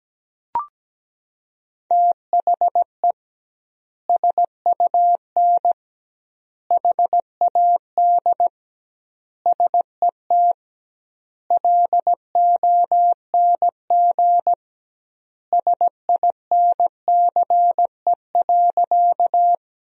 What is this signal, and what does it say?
Morse code, heard as a keyed tone, about 17 wpm, 700 Hz. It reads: THE SUN HAD SET LONG SINCE.
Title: 2x Word-Spacing - Sentences from Top 300 Words - 17wpm